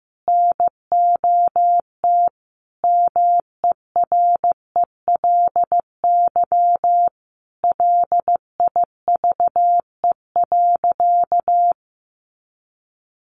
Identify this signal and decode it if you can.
Morse code, heard as a keyed tone, about 15 wpm, 700 Hz. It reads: NOT MERELY LIVE.
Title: Sentences from Top 1400 Words 15wpm